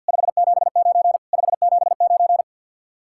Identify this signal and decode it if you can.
Morse code, heard as a keyed tone, about 63 wpm, 700 Hz. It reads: 579 579